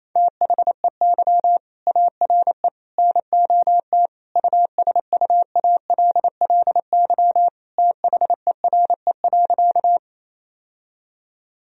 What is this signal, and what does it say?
Morse code, heard as a keyed tone, about 28 wpm, 700 Hz. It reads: THEY ARE NOT USUALLY THERE.
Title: Sentences from Top 1000 Words - Review Part 3 - 28wpm